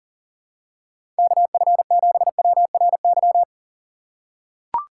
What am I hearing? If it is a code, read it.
Morse code, heard as a keyed tone, about 40 wpm, 700 Hz. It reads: KF7WRY
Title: USA Call Signs - 6 Characters - 2x Character Spacing - 40wpm